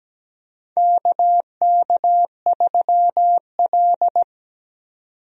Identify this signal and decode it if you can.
Morse code, heard as a keyed tone, about 17 wpm, 700 Hz. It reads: KK3L
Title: USA Call Signs - 4 Characters - 17wpm